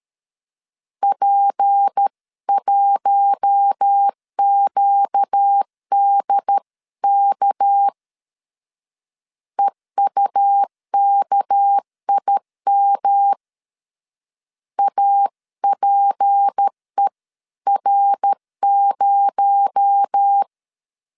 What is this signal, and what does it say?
Morse code, heard as a keyed tone, about 13 wpm, 800 Hz. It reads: P1QDK EUKIM APER0
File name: CW / DET / 1 6 1500C S A 12wpm 3440